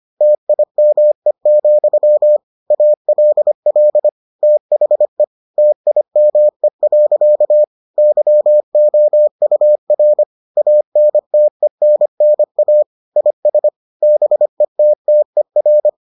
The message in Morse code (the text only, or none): TIME, ALL THE TIME. YOUR ANTENNA IS BETTER